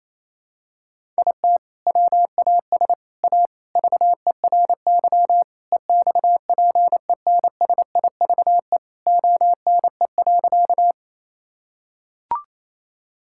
Morse code, heard as a keyed tone, about 27 wpm, 700 Hz. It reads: IT WAS A VERY EXPENSIVE ONE.
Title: Sentences from Top 1000 Words - Review Part 6 - 28wpm